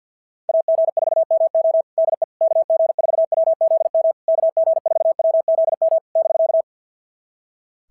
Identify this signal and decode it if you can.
Morse code, heard as a keyed tone, about 46 wpm, 650 Hz. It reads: AG4MO DE KG4WZM KG4WZM <BK>
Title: SOTA QSO Practice 50wpm